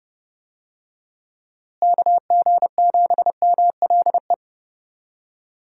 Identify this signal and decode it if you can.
Morse code, heard as a keyed tone, about 29 wpm, 700 Hz. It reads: KG7MLE
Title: USA Call Signs - 6 Characters - Repeated 2x - 30wpm